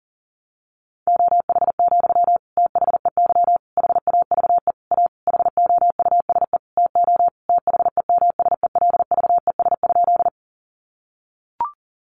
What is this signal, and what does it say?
Morse code, heard as a keyed tone, about 40 wpm, 700 Hz. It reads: OH, THEY HAVE A HOUSE TO THEMSELVES?
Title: Sentences from Top 800 Words 40wpm